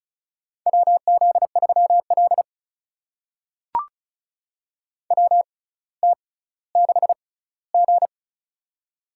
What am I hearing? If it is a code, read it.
Morse code, heard as a keyed tone, about 36 wpm, spaced as for 18 wpm, 700 Hz. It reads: WZ3L WT6G
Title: USA Call Signs - 4 Characters - 6x Character Spacing - 35wpm